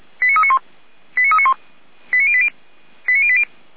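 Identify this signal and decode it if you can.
Five-tone selective call (ZVEI), heard as quick runs of five tones, tones of about 70 ms. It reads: 89391 89391 89080 89080
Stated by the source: Five-tone ZVEI selective call received on an SDR.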